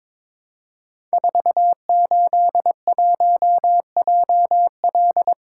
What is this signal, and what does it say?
Morse code, heard as a keyed tone, about 23 wpm, 700 Hz. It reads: 481JL
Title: Code Groups - 5 Letter:Numbers - 22wpm